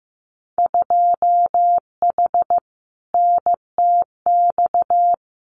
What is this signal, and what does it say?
Morse code, heard as a keyed tone, about 15 wpm, 700 Hz. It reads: 2H NTX
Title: Winter Field Day Exchanges Encore 15wpm